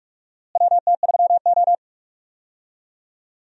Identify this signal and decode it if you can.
Morse code, heard as a keyed tone, about 47 wpm, 700 Hz. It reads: WT3O
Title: USA Call Signs - 4 Characters - Repeated 2x - 45wpm